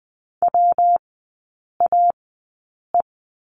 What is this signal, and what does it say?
Morse code, heard as a keyed tone, about 20 wpm, spaced as for 8 wpm, 700 Hz. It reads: WAE
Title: 178 Lesson 29e - Slash - Three Characters 20wpm